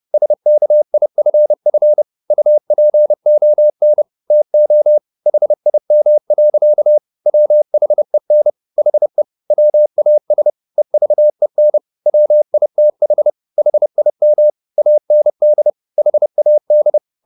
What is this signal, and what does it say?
Morse code, heard as a keyed tone, about 30 wpm, 600 Hz. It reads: SKIFF UPON TO HIM. WHEN HE WAS EVEN WITH HIM AND HAD